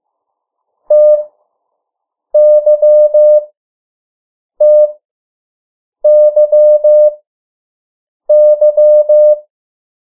Morse code, heard as a keyed tone, about 15 wpm, spaced as for 6 wpm, 600 Hz. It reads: TYTYY
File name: morse mp3 / 03 urok 2